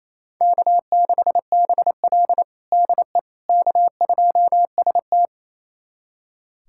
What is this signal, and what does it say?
Morse code, heard as a keyed tone, about 28 wpm, 700 Hz. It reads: K6BL DE K2ST
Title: USA Callsign DE Callsign - 4 Characters - 28wpm